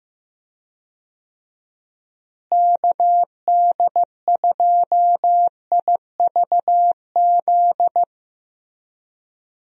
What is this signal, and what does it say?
Morse code, heard as a keyed tone, about 15 wpm, 700 Hz. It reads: KD2IVZ
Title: USA Call Signs - 6 Characters - Repeated 2x - 15wpm